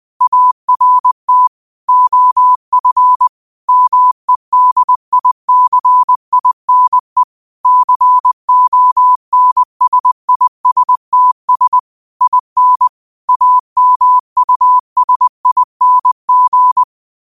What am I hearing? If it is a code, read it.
Morse code, heard as a keyed tone, about 20 wpm, 1000 Hz. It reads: ART OF MEDICINE CONSISTS IN AMUSING
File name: qotdfiles / 20wpm